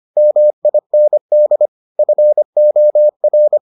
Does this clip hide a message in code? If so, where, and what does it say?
Yes: Morse code from 0.2 to 3.6 s, heard as a keyed tone, about 25 wpm, 600 Hz. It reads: MIND FOR